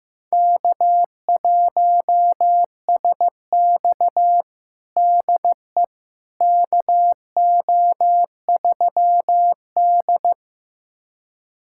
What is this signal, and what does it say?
Morse code, heard as a keyed tone, about 15 wpm, 700 Hz. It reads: K1SX DE KO3D
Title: USA Callsign DE Callsign - 4 Characters - 15wpm